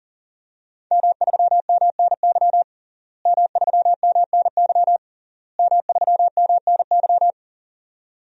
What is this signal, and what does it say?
Morse code, heard as a keyed tone, about 40 wpm, 700 Hz. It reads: M3MNY M3MNY M3MNY
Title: DXCC-100 Call Signs - 5 Characters - Repeated 3x - 40wpm